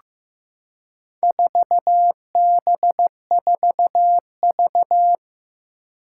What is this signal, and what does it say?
Morse code, heard as a keyed tone, about 15 wpm, 700 Hz. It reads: 4B4V
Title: Four Challenging Letter-Number Combinations 15wpm